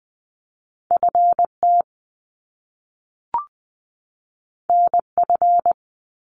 Morse code, heard as a keyed tone, about 20 wpm, 700 Hz. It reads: FT NF
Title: Top Two Letter Combinations in Words 20wpm